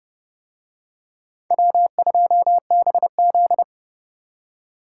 Morse code, heard as a keyed tone, about 28 wpm, 700 Hz. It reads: W2BZ